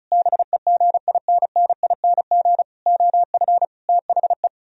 Morse code, heard as a keyed tone, about 44 wpm, 700 Hz. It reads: BEGINNING OF THE
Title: Speed-Racing - Sets of 5 Words 35wpm